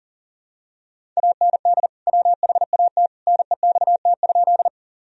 Morse code, heard as a keyed tone, about 40 wpm, 700 Hz. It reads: AND WHAT NEXT?
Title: Sentences from Top 1000 Words - Review Part 1 - 40wpm